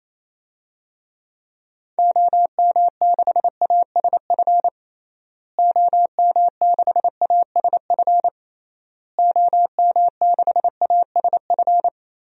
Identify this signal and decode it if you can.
Morse code, heard as a keyed tone, about 28 wpm, 700 Hz. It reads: OM6ASF OM6ASF OM6ASF
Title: DXCC-100 Call Signs - 6 Characters - Repeated 3x - 28wpm